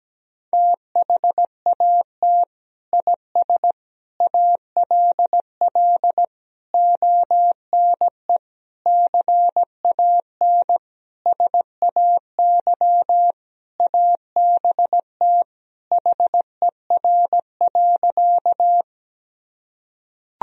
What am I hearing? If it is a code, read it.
Morse code, heard as a keyed tone, about 17 wpm, 700 Hz. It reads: THAT IS ALL ONE CAN SAY ABT HER.
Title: Sentences from Top 200 Words and Abbreviations 17wpm